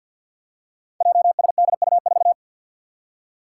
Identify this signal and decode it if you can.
Morse code, heard as a keyed tone, about 53 wpm, 700 Hz. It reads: JSDU4
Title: Five Challenging Letter-Number Combinations 50wpm